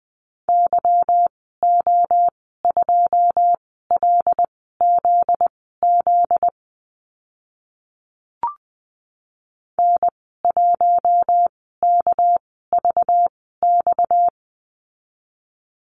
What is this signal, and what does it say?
Morse code, heard as a keyed tone, about 20 wpm, spaced as for 15 wpm, 700 Hz. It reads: YO2LZZ N1KVX
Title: 245 Lesson 39g - Letter z - Call Signs 20wpm